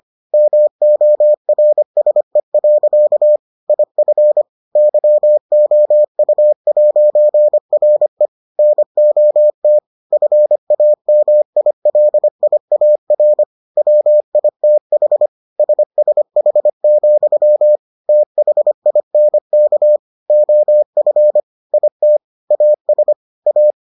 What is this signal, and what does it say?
Morse code, heard as a keyed tone, about 25 wpm, 600 Hz. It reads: MORSE. IF YOU'RE NOT FAMILIAR WITH SSH, THINK OF IT AS A